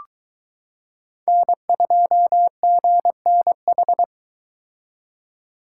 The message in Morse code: N2GNH

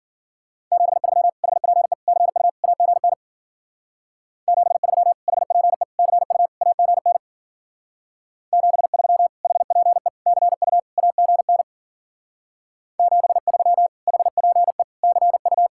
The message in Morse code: 73 HPE CU AGN 73 HPE CU AGN 73 HPE CU AGN 73 HPE CU

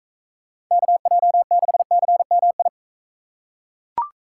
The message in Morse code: KJ6CMI

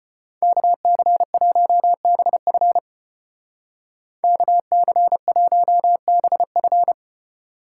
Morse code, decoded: KC1BF KC1BF

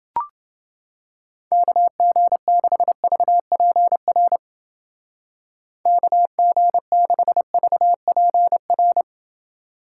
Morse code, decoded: KG6VPR KG6VPR

